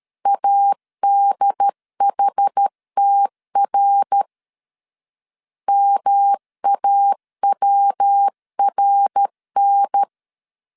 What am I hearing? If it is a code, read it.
Morse code, heard as a keyed tone, about 13 wpm, 800 Hz. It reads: ADHTR MAWRN
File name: CW / DET / 1 3 1500C S A 12wpm 2905